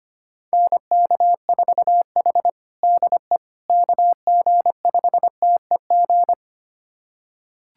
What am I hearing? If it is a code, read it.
Morse code, heard as a keyed tone, about 25 wpm, 700 Hz. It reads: NK4H DE KG5TEG
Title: USA Callsign DE Callsign 25wpm